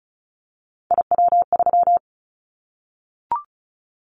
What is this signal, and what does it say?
Morse code, heard as a keyed tone, about 34 wpm, 700 Hz. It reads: IW3
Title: Callsign Prefixes for Top 25 DXCC - Repeated 3x - 35wpm